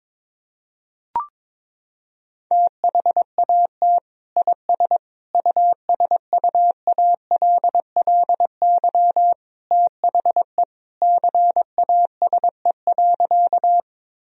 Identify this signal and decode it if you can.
Morse code, heard as a keyed tone, about 22 wpm, 700 Hz. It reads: THAT IS USUALLY THE CASE.
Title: Sentences from Top 1000 Words - Review Part 3 - 22wpm